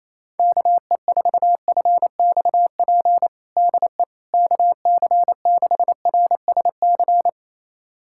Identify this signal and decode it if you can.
Morse code, heard as a keyed tone, about 28 wpm, 700 Hz. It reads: KE4FXP DE KC6RSC
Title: USA Callsign DE Callsign - 6 Characters - 28wpm